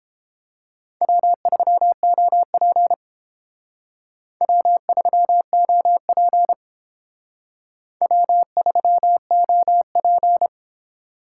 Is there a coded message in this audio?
Yes: Morse code from 1.0 to 10.5 s, heard as a keyed tone, about 29 wpm, 700 Hz. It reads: W3OP W3OP W3OP